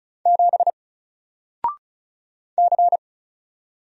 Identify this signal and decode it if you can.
Morse code, heard as a keyed tone, about 36 wpm, 700 Hz. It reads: 7 C
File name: Single-Letter-Number-35wpm